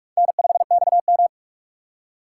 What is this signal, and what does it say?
Morse code, heard as a keyed tone, about 60 wpm, spaced as for 53 wpm, 700 Hz. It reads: N5XM
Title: Speed Racing - USA 1x2 Call Signs 45wpm